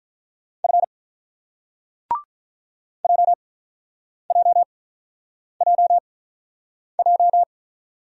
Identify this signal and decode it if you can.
Morse code, heard as a keyed tone, about 49 wpm, spaced as for 9 wpm, 690 Hz. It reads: F JJJJ